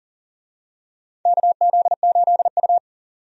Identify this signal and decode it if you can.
Morse code, heard as a keyed tone, about 40 wpm, 690 Hz. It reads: KZ8U
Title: USA Call Signs - 4 Characters - 40wpm